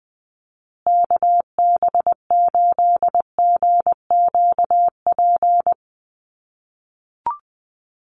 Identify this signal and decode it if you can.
Morse code, heard as a keyed tone, about 20 wpm, 700 Hz. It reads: KB8GQP
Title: USA Call Signs Repeated 3x 20wpm